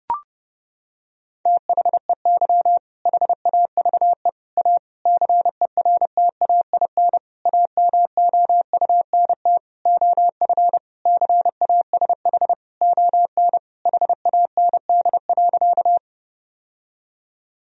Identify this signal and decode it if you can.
Morse code, heard as a keyed tone, about 32 wpm, 700 Hz. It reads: THEY HAVE A CERTAIN AMOUNT OF CASH ON HAND.
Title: Sentences from Top 1000 Words - Review Part 4 - 30wpm